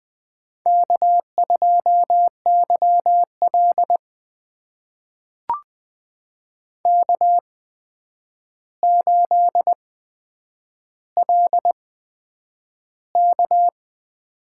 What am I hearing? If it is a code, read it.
Morse code, heard as a keyed tone, about 20 wpm, spaced as for 9 wpm, 700 Hz. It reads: K2YL K8LK
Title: USA Call Signs - 4 Characters - 8x Character Spacing - 20wpm